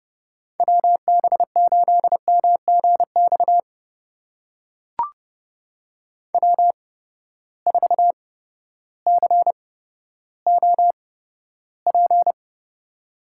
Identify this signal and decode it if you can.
Morse code, heard as a keyed tone, about 30 wpm, spaced as for 14 wpm, 700 Hz. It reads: WB8MGX W4COP
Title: USA Call Signs - 8x Character Spacing - 30wpm